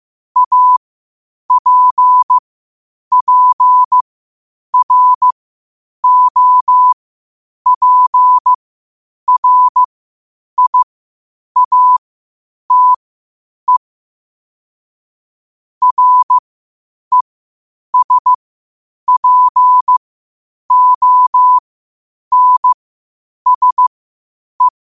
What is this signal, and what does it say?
Morse code, heard as a keyed tone, about 15 wpm, spaced as for 8 wpm, 1000 Hz. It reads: APPROPRIATE RESPONSE
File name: qotdfiles / 8wpm